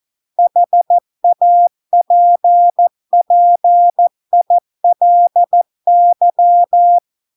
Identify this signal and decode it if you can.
Morse code, heard as a keyed tone, about 14 wpm, 700 Hz. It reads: HAPPILY